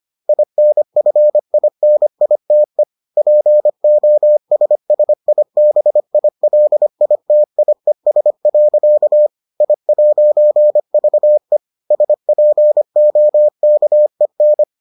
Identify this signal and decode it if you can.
Morse code, heard as a keyed tone, about 25 wpm, 600 Hz. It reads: INFINITE POSSIBILITIES. I'VE SPOKEN